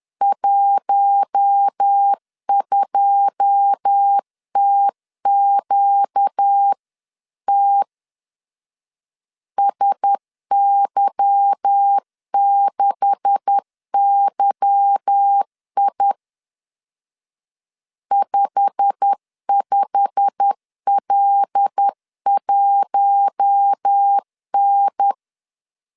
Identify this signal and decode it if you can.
Morse code, heard as a keyed tone, about 10 wpm, 800 Hz. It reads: 12TQT SY6YI 55L1N
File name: CW / DET / 1 9 1500C S A 10wpm 4511